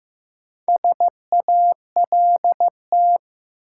Morse code, heard as a keyed tone, about 15 wpm, 700 Hz. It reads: SALT
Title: Sentences from Top 800 Words - New Word Drill - 15wpm